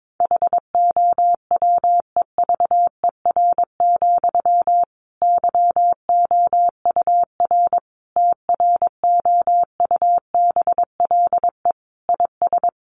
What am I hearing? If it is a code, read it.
Morse code, heard as a keyed tone, about 22 wpm, 700 Hz. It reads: HOWEVER, YOUR TROUBLE IS
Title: Sentences from Top 1200 Words 22wpm